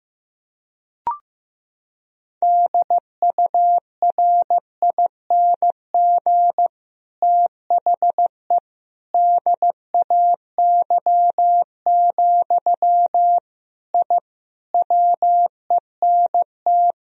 Morse code, heard as a keyed tone, about 15 wpm, 700 Hz. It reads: DURING THE DAY, I WENT
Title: Sentences from Top 1000 Words - Review Part 2 - 15wpm